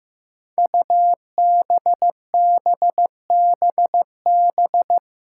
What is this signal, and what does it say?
Morse code, heard as a keyed tone, about 15 wpm, 700 Hz. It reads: UBBBB
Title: Five Challenging Letter-Number Combinations 15wpm